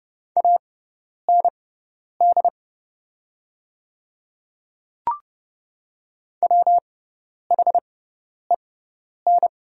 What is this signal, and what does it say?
Morse code, heard as a keyed tone, about 30 wpm, spaced as for 10 wpm, 700 Hz. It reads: AND WHEN
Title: 6x Character Spacing - Top 200 Words - 30wpm